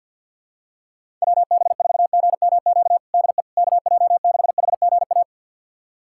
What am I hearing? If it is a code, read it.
Morse code, heard as a keyed tone, about 50 wpm, 700 Hz. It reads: WB4GMQ DE KJ6HGA
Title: USA Callsign DE Callsign - 6 Characters - 50wpm